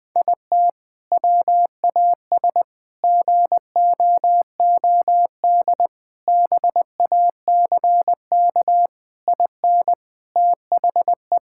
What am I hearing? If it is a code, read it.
Morse code, heard as a keyed tone, about 20 wpm, 700 Hz. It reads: IT WAS GOOD BACK IN THE
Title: Sentences with 3-Word Phrases - 20wpm